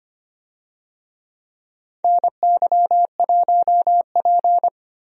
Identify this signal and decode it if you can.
Morse code, heard as a keyed tone, about 25 wpm, 700 Hz. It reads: NY1P
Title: USA Call Signs - 4 Characters - Repeated 2x - 25wpm